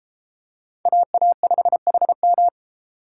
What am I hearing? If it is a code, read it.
Morse code, heard as a keyed tone, about 29 wpm, 700 Hz. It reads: AA5HM